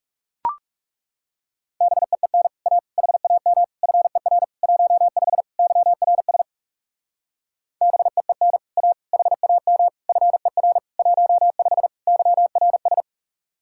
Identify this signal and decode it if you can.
Morse code, heard as a keyed tone, about 41 wpm, 700 Hz. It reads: BEEN A HAM FER 15 YRS BEEN A HAM FER 15 YRS